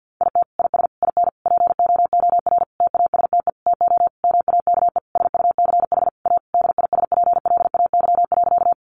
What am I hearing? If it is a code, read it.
Morse code, heard as a keyed tone, about 50 wpm, 700 Hz. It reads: IT IS IN POOR TASTE TO MAKE SUCH A DISPLAY.
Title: Sentences from Top 1200 Words 50wpm